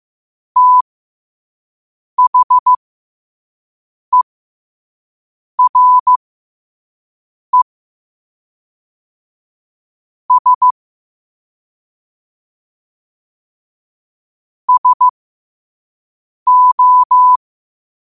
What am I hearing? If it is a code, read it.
Morse code, heard as a keyed tone, about 15 wpm, spaced as for 5 wpm, 1000 Hz. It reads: THERE S SO
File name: qotdfiles / 5wpm